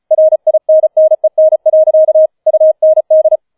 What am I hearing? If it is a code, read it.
Morse code, heard as a keyed tone, about 35 wpm, 630 Hz. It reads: RINNEN. UND